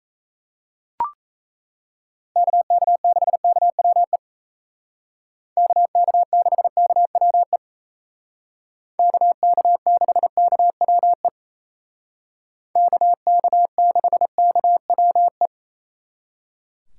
Morse code, heard as a keyed tone, about 35 wpm, 700 Hz. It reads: KK6KWE KK6KWE KK6KWE KK6KWE
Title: Speed Racing - USA 2x3 Call Signs 28wpm